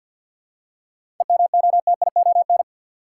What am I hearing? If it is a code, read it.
Morse code, heard as a keyed tone, about 53 wpm, 700 Hz. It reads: EMOTION
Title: Seven Letter Words - 50wpm